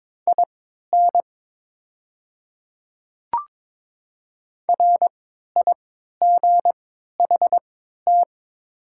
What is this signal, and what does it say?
Morse code, heard as a keyed tone, about 23 wpm, spaced as for 13 wpm, 700 Hz. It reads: IN RIGHT
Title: 3x Character Spacing - Top 200 Words - 22wpm